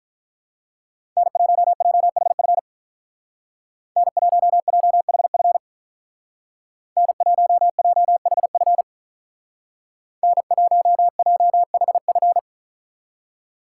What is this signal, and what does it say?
Morse code, heard as a keyed tone, about 43 wpm, 700 Hz. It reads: N1JHF N1JHF N1JHF N1JHF
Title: Speed Racing - USA Call Signs 35wpm